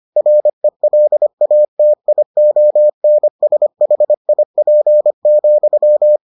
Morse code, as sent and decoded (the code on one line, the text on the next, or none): .-. .-.. .- - .. --- -. ... .... .. .--. --..--
RELATIONSHIP,